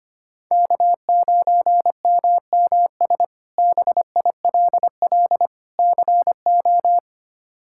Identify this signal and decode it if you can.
Morse code, heard as a keyed tone, about 25 wpm, 700 Hz. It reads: K9MMS BILL CO
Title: K1USN SST QSO Practice - 25wpm